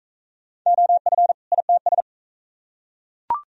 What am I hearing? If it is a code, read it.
Morse code, heard as a keyed tone, about 37 wpm, 700 Hz. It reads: OF ITS